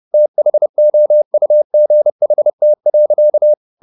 Morse code, heard as a keyed tone, about 30 wpm, 600 Hz. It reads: THOUGHT.